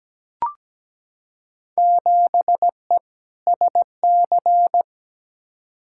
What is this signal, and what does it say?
Morse code, heard as a keyed tone, about 17 wpm, 700 Hz. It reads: 7E SC
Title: Field Day Exchanges Encore - 17wpm